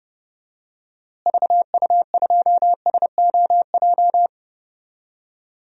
Morse code, heard as a keyed tone, about 32 wpm, 700 Hz. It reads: VU2SOJ
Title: DXCC-100 Call Signs - 6 Characters - Repeated 2x - 30wpm